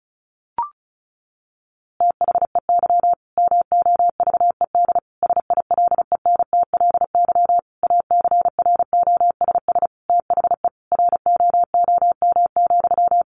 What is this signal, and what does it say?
Morse code, heard as a keyed tone, about 34 wpm, 700 Hz. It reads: THEY MOVED SILENTLY ACROSS THE ROOM,
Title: A Sound Of Thunder 35wpm